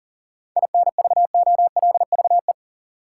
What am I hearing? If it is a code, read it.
Morse code, heard as a keyed tone, about 40 wpm, 700 Hz. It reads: INVOLVE
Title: Sentences from Top 1200 Words - New Word Drill - 40wpm